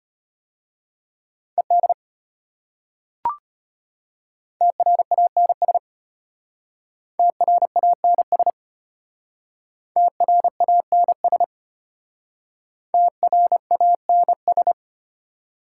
Morse code, heard as a keyed tone, about 32 wpm, 700 Hz. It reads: ED TRANS TRANS TRANS TRANS